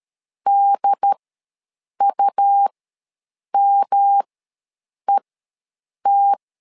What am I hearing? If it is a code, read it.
Morse code, heard as a keyed tone, about 13 wpm, spaced as for 7 wpm, 800 Hz. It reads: DUMET
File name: LEZIONI / 3 MWRO / 3 MW ws 12wpm 1816